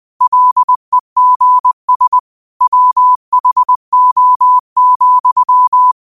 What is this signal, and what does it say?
Morse code, heard as a keyed tone, about 20 wpm, 1000 Hz. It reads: LEGS WHO,
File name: qotdfiles / 20wpm